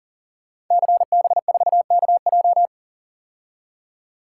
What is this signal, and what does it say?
Morse code, heard as a keyed tone, about 42 wpm, 700 Hz. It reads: CB4KJ
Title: Code Groups - 5 Letter:Numbers - 40wpm